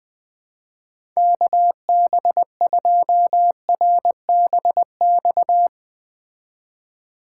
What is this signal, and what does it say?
Morse code, heard as a keyed tone, about 20 wpm, 700 Hz. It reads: KB2RBX